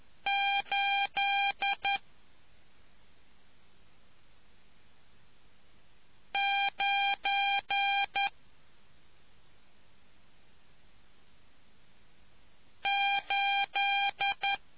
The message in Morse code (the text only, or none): 8 9 8